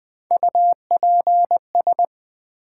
Morse code, heard as a keyed tone, about 20 wpm, 700 Hz. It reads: UPS
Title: Build A Word - 3 Letters At A Time - 8-Letter Words - 20wpm